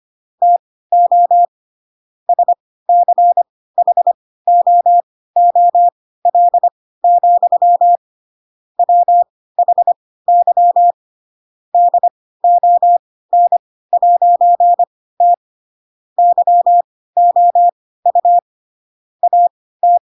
Morse code, heard as a keyed tone, about 25 wpm, spaced as for 16 wpm, 700 Hz. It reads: TO SCHOOL, WHY DON'T YOU AT
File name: Pinocchio / Pinocchio ch4 10 25 ik1hge